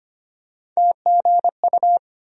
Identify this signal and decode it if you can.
Morse code, heard as a keyed tone, about 25 wpm, 700 Hz. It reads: TGU
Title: USA Call Sign Postfixes - 3 Letters 25wpm